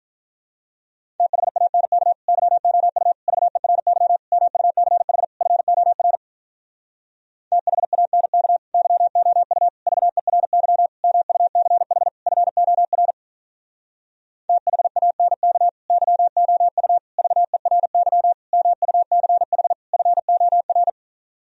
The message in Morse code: THANK YOU VERY MUCH FOR THANK YOU VERY MUCH FOR THANK YOU VERY MUCH FOR

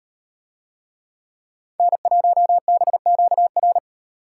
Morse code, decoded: N1BQR

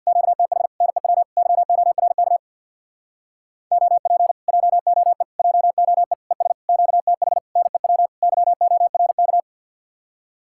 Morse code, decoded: QTH NEW YORK OP JOE JOE ES QTH NEW YORK

